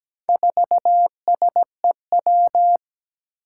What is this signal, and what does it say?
Morse code, heard as a keyed tone, about 17 wpm, 700 Hz. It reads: 4SEW
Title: Four Challenging Letter-Number Combinations 17wpm